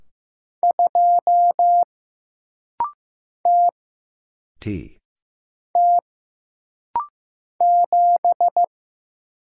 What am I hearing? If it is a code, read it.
Morse code, heard as a keyed tone, about 15 wpm, 700 Hz. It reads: 2 T T 7
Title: Single Letter-Number Alt - Rapid-Fire 15wpm